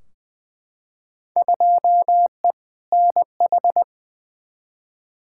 Morse code, heard as a keyed tone, about 20 wpm, 700 Hz. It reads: 2E NH
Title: Field Day Exchanges - 20wpm